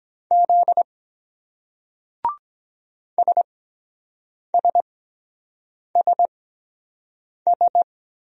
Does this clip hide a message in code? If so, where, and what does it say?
Yes: Morse code from 0.3 to 7.8 s, heard as a keyed tone, about 23 wpm, spaced as for 7 wpm, 700 Hz. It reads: Z SSSS